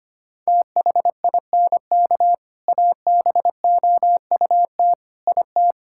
Morse code, heard as a keyed tone, about 25 wpm, 700 Hz. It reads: THINK ABOUT IT
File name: practice / 25wpm / Sentences with 3-Word Phrases - 25wpm